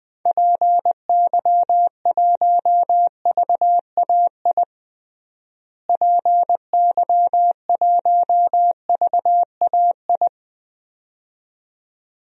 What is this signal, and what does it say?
Morse code, heard as a keyed tone, about 20 wpm, 700 Hz. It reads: PY1VAI PY1VAI
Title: DXCC-100 Call Signs - 6 Characters - Repeated 2x - 20wpm